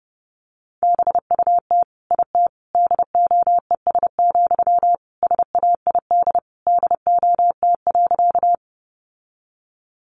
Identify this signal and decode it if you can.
Morse code, heard as a keyed tone, about 28 wpm, 700 Hz. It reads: BUT IT DOES, SAID DOT.
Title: Sentences from Top 1500 Words 30wpm